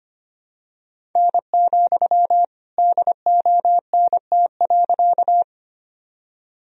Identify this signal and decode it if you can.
Morse code, heard as a keyed tone, about 25 wpm, 700 Hz. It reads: N, DONT.
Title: Sentences from Top 300 Words and Abbreviations 25wpm